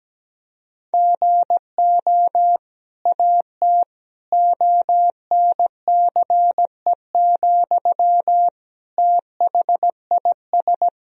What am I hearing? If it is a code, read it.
Morse code, heard as a keyed tone, about 17 wpm, 700 Hz. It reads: GO AT ONCE, THIS
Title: Sentences from Top 1000 Words - Review Part 1 - 17wpm